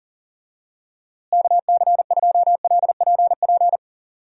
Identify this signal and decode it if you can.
Morse code, heard as a keyed tone, about 40 wpm, 700 Hz. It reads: KC2LPP
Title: USA Call Signs - 6 Characters - Repeated 3x - 40wpm